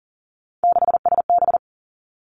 Morse code, heard as a keyed tone, about 38 wpm, 700 Hz. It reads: 6SB
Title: Three Challenging Letter-Number Combinations 40wpm